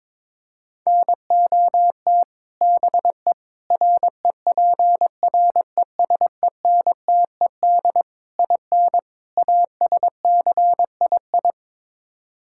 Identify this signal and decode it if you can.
Morse code, heard as a keyed tone, about 22 wpm, 700 Hz. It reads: NOT BE REPRESENTED IN ASCII
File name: practice / 22wpm / Build A Phrase - 5-Word Phrases - 22wpm